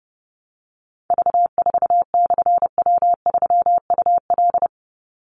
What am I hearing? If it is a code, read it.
Morse code, heard as a keyed tone, about 30 wpm, 700 Hz. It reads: V4/W3UL
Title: Call Signs - 30wpm